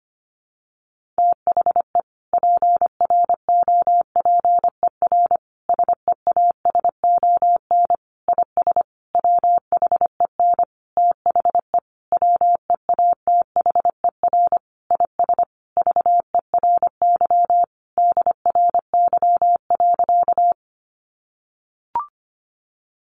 Morse code, heard as a keyed tone, about 25 wpm, 700 Hz. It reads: THE PROPER SEASON IS WHEN THE WEATHER IS VERY DRY.
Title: Sentences from Top 1400 Words 25wpm